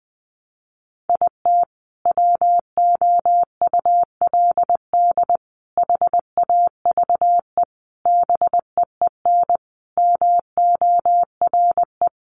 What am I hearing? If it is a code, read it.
Morse code, heard as a keyed tone, about 19 wpm, 700 Hz. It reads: IT WOULD HAVE BEEN MORE
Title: Sentences from Top 400 Words 20wpm